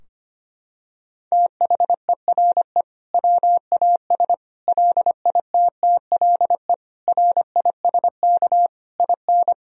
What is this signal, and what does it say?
Morse code, heard as a keyed tone, about 25 wpm, 700 Hz. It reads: THERE WAS LITTLE RISK IN
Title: Sentences from Top 1000 Words - Review Part 4 - 25wpm